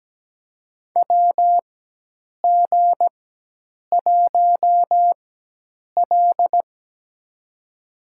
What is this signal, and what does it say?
Morse code, heard as a keyed tone, about 17 wpm, spaced as for 8 wpm, 700 Hz. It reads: WG1L
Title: USA Call Signs - 4 Characters - 4x Character Spacing - 17wpm